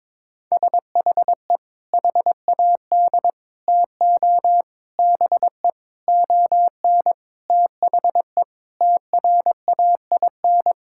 Morse code, heard as a keyed tone, about 22 wpm, 700 Hz. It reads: SHE HAD TO BE ON THE TRAIN
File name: practice / 22wpm / Sentences with 4-Word Phrases - 22wpm